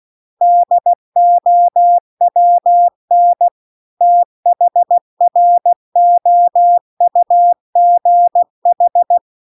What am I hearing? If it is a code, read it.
Morse code, heard as a keyed tone, about 16 wpm, 700 Hz. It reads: DOWN THROUGH
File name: CW-The-Big-Twoheaded-River-2-16WPM